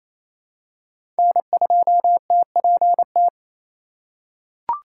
Morse code, heard as a keyed tone, about 28 wpm, 700 Hz. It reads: N2TPT
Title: USA Call Signs - 4x Character Spacing - 28wpm